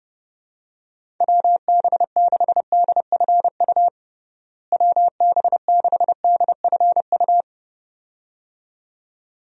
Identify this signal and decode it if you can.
Morse code, heard as a keyed tone, about 30 wpm, 700 Hz. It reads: WB6DFU WB6DFU
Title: USA Call Signs - 6 Characters - Repeated 2x - 30wpm